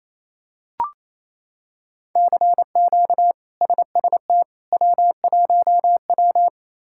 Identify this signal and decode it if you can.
Morse code, heard as a keyed tone, about 27 wpm, 700 Hz. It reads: CQ SST W1W
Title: K1USN SST QSO Practice - 28wpm